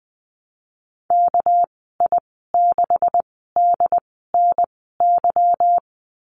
Morse code, heard as a keyed tone, about 20 wpm, 700 Hz. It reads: K I 6 D N Y
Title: 231 Lesson 37g - Letter x - Call Signs 20wpm